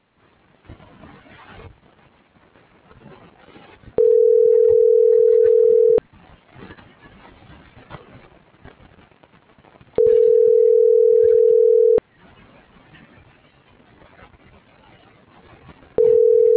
Ringback tone (North America), 440 + 480 Hz.